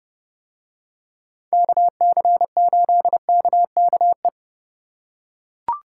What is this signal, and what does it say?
Morse code, heard as a keyed tone, about 30 wpm, 700 Hz. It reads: KC8KKE